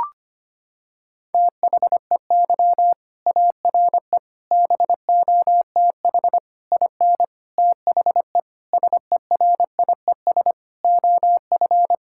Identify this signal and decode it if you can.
Morse code, heard as a keyed tone, about 25 wpm, 700 Hz. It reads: THEY ARE BOTH IN THE SERIES OF
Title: Sentences from Top 1000 Words - Review Part 5 - 25wpm